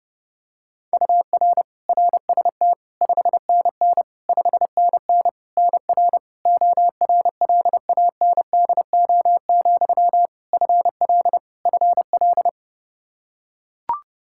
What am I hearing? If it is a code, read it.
Morse code, heard as a keyed tone, about 30 wpm, 700 Hz. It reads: UR RST 5NN 5NN NR ORLANDO, FL FL
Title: Practice QSO - Exchange 1 - Beginning - 30wpm